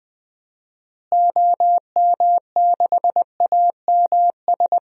Morse code, heard as a keyed tone, about 20 wpm, 700 Hz. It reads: OM6AMS
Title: DXCC-100 Call Signs - 6 Characters - Repeated 2x - 20wpm